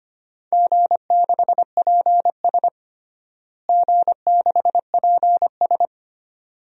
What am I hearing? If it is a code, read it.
Morse code, heard as a keyed tone, about 25 wpm, 700 Hz. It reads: G6PS G6PS
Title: DXCC-100 Call Signs - 4 Characters - Repeated 2x - 25wpm